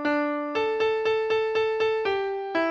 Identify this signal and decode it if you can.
There is no code.